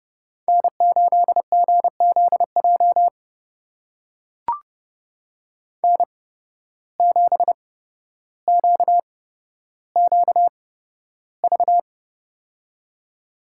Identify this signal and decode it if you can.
Morse code, heard as a keyed tone, about 29 wpm, spaced as for 13 wpm, 700 Hz. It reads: N8GZJ N7QQV
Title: USA Call Signs - 5 Characters - 8x Character Spacing - 30wpm